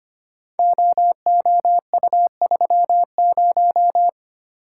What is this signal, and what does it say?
Morse code, heard as a keyed tone, about 25 wpm, 700 Hz. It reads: OOU30
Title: Code Groups - 5 Letter:Numbers - 25wpm